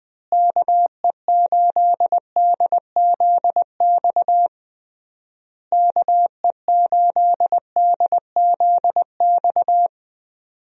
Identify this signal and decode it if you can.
Morse code, heard as a keyed tone, about 21 wpm, 700 Hz. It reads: KE8DZX KE8DZX
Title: USA Call Signs - 6 Characters - Repeated 3x - 20wpm